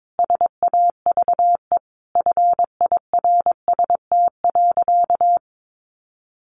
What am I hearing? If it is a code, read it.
Morse code, heard as a keyed tone, about 22 wpm, 700 Hz. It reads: SAVE FIRST.
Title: Sentences from Top 1100 Words 22wpm